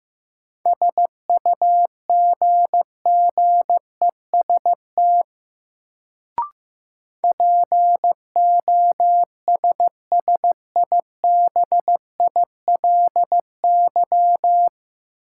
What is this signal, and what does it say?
Morse code, heard as a keyed tone, about 15 wpm, 700 Hz. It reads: SUGGEST POSSIBILY
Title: Sentences from Top 1300 Words - New Word Drill Alt - 15wpm